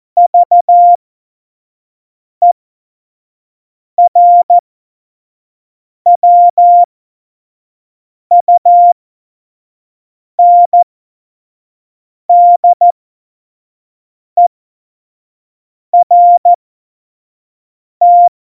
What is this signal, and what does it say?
Morse code, heard as a keyed tone, about 14 wpm, spaced as for 5 wpm, 700 Hz. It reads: VERWUNDERT
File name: Morsekurs / Witze / Witz 15. Dezember, F70